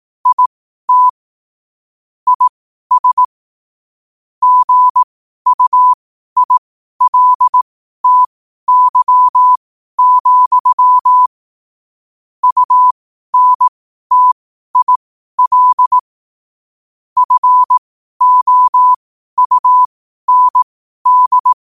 Morse code, heard as a keyed tone, about 18 wpm, spaced as for 12 wpm, 1000 Hz. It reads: IT IS GUILTY, UNTIL FOUND
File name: qotdfiles / 12wpm